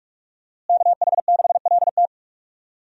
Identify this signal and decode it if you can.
Morse code, heard as a keyed tone, about 46 wpm, 700 Hz. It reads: KH6LT
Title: USA Call Signs - 5 Characters - Repeated 2x - 45wpm